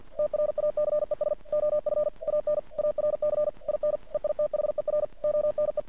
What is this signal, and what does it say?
Morse code, heard as a keyed tone, about 48 wpm, 600 Hz. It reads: TRACES OF AN ARC IN EITHER ONE
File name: archived / CRSnet